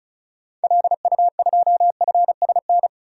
Morse code, heard as a keyed tone, about 36 wpm, 700 Hz. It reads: LU2FSN
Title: DXCC-100 Call Signs - 6 Characters - 35wpm